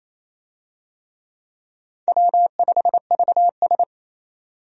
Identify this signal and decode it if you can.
Morse code, heard as a keyed tone, about 28 wpm, 700 Hz. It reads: W5VS
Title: USA Call Signs - 4 Characters - 28wpm